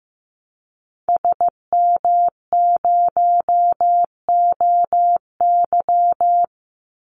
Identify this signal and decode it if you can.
Morse code, heard as a keyed tone, about 15 wpm, 700 Hz. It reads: SM0OY
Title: Call Signs Repeated 3x - 15wpm